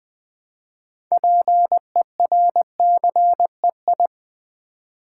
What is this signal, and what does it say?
Morse code, heard as a keyed tone, about 20 wpm, 700 Hz. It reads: PERCEI
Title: Build A Word - 2 Letters At A Time - 8-Letter Words - 20wpm